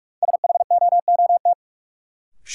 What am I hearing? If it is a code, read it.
Morse code, heard as a keyed tone, about 41 wpm, 700 Hz. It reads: SHOOT
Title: Sentences from Top 1200 Words - New Word Drill Alt - 45wpm